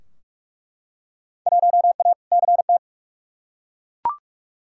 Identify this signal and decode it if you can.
Morse code, heard as a keyed tone, about 44 wpm, 700 Hz. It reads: 1A CT
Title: Field Day Exchanges Encore - 45wpm